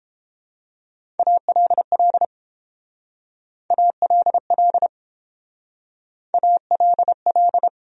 Morse code, heard as a keyed tone, about 29 wpm, 700 Hz. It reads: ALL ALL ALL